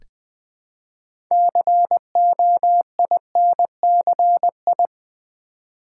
Morse code, heard as a keyed tone, about 20 wpm, 700 Hz. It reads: COINCI
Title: Build A Word - 2 Letters At A Time - 8-Letter Words - 20wpm